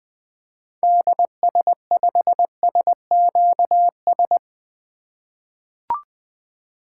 Morse code, heard as a keyed tone, about 20 wpm, 700 Hz. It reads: DS5SQS
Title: 2023 SOTA Activator Call Signs - 20wpm